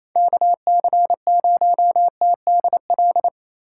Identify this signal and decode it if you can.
Morse code, heard as a keyed tone, about 28 wpm, 700 Hz. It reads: KC0TDL
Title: USA Call Signs - 3x Character Spacing - 28wpm